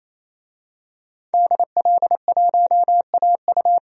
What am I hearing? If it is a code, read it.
Morse code, heard as a keyed tone, about 28 wpm, 700 Hz. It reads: DL1AU
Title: DXCC-100 Call Signs - 5 Characters - 28wpm